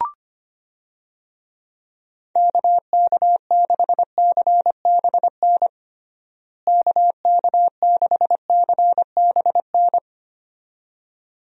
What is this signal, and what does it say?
Morse code, heard as a keyed tone, about 25 wpm, 700 Hz. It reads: KK6CBN KK6CBN
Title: USA Call Signs - 6 Characters - Repeated 2x - 25wpm